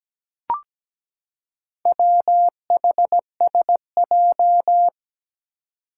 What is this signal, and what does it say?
Morse code, heard as a keyed tone, about 17 wpm, 700 Hz. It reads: WHSJ
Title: Four Challenging Letter-Number Combinations 17wpm